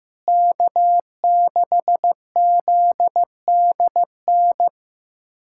Morse code, heard as a keyed tone, about 15 wpm, 700 Hz. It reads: K6ZDN
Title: USA Callsign DE Callsign - 5 Characters - 15wpm